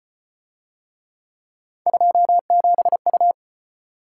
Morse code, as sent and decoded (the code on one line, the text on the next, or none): ..--- --... ..-
27U